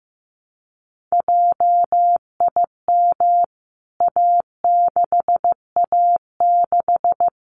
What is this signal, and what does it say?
Morse code, heard as a keyed tone, about 15 wpm, 700 Hz. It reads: JIM A6A6
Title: CWT Exchange 15wpm